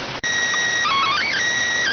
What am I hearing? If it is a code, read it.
SSTV (slow-scan television), PD 290.